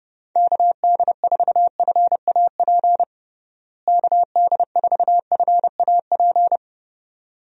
Morse code, heard as a keyed tone, about 30 wpm, 700 Hz. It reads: KD4FAP KD4FAP
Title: USA Call Signs - 6 Characters - Repeated 2x - 30wpm